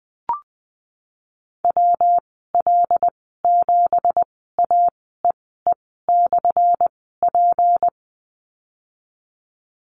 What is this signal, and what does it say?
Morse code, heard as a keyed tone, about 20 wpm, spaced as for 15 wpm, 700 Hz. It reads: WL7AEE/P
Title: 179 Lesson 29g - Slash - Call Signs 20wpm